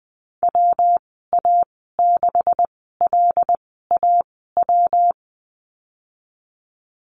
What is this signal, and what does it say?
Morse code, heard as a keyed tone, about 20 wpm, spaced as for 15 wpm, 700 Hz. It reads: WA6LAW
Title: 161 Lesson 26g - Letter g - Call Signs 20wpm